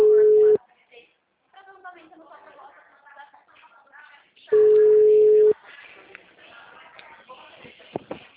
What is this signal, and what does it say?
Ringback tone (Europe, 425 Hz), with speech.